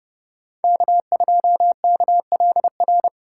Morse code, heard as a keyed tone, about 30 wpm, 700 Hz. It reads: K2KLR
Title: USA Call Signs - 2x Character Spacing - 30wpm